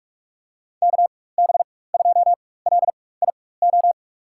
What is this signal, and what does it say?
Morse code, heard as a keyed tone, about 42 wpm, spaced as for 21 wpm, 700 Hz. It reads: KB2LIO
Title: USA Call Signs - 6 Characters - 4x Character Spacing - 45wpm